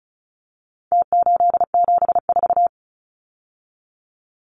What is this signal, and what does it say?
Morse code, heard as a keyed tone, about 34 wpm, 700 Hz. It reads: T874
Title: Numbers - Contest Style - 4-Digit 35wpm